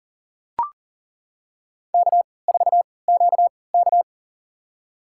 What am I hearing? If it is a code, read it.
Morse code, heard as a keyed tone, about 38 wpm, spaced as for 22 wpm, 700 Hz. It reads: K4QK
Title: USA Call Signs - 3x Character Spacing - 40wpm